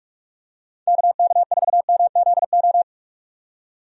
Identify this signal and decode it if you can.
Morse code, heard as a keyed tone, about 45 wpm, 700 Hz. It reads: KK4MZO
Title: USA Call Signs - 6 Characters - 45wpm